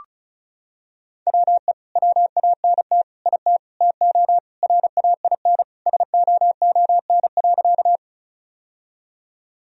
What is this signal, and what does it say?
Morse code, heard as a keyed tone, about 34 wpm, 700 Hz. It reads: WE WANT IT TO RAIN SOON.
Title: Sentences from Top 1000 Words - Review Part 1 - 35wpm